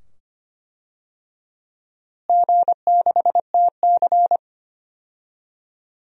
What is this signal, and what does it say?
Morse code, heard as a keyed tone, about 25 wpm, 700 Hz. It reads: G6TC
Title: DXCC-100 Call Signs - 4 Characters - Repeated 2x - 25wpm